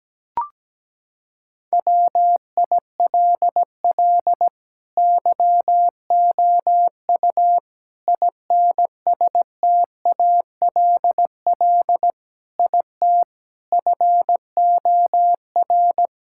Morse code, heard as a keyed tone, about 17 wpm, 700 Hz. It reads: WILL YOU INSTALL IT FOR
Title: Sentences from Top 1600 Words 17wpm